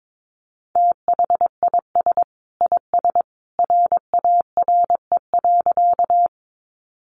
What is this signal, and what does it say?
Morse code, heard as a keyed tone, about 22 wpm, 700 Hz. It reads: THIS IS RARE.
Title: Sentences from Top 1100 Words 22wpm